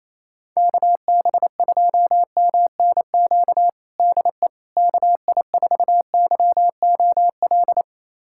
Morse code, heard as a keyed tone, about 28 wpm, 700 Hz. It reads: KB2MNQ DE KI4YOL
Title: USA Callsign DE Callsign - 6 Characters - 28wpm